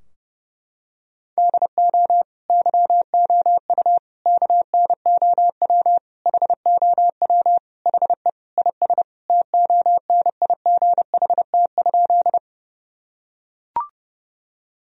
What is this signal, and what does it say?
Morse code, heard as a keyed tone, about 32 wpm, 700 Hz. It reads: DO YOU KNOW HOW HE IS TONIGHT?
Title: Sentences from Top 1000 Words - Review Part 3 - 30wpm